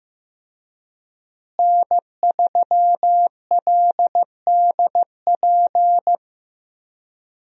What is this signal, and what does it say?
Morse code, heard as a keyed tone, about 15 wpm, 700 Hz. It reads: N3LDP
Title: USA Call Signs - 5 Characters - Repeated 3x - 15wpm